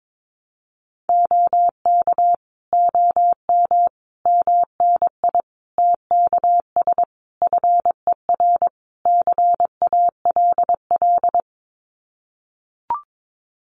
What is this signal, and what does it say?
Morse code, heard as a keyed tone, about 22 wpm, 700 Hz. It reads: OK OM MNI TKS FER CALL
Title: Common QSO Phrases 22wpm